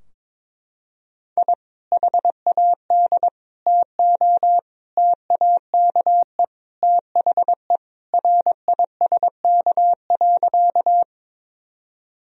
Morse code, heard as a keyed tone, about 21 wpm, 700 Hz. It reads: I HAD TO TAKE THE RISK.
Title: Sentences from Top 1000 Words - Review Part 4 - 22wpm